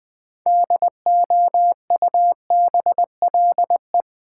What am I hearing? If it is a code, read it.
Morse code, heard as a keyed tone, about 20 wpm, 700 Hz. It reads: DOUBLE
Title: Sentences from Top 1400 Words - New Word Drill Alt - 20wpm